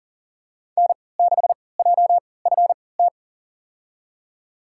Morse code, heard as a keyed tone, about 40 wpm, spaced as for 23 wpm, 700 Hz. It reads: N6JFT
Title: USA Call Signs - 5 Characters - 3x Character Spacing - 40wpm